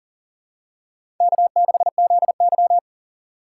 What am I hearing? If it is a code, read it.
Morse code, heard as a keyed tone, about 39 wpm, 700 Hz. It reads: K6ZY
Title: USA Call Signs - 4 Characters - 6x Character Spacing - 40wpm